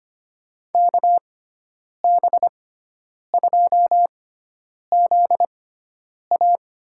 Morse code, heard as a keyed tone, about 25 wpm, spaced as for 9 wpm, 700 Hz. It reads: KB2ZA
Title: USA Call Signs - 6x Character Spacing - 25wpm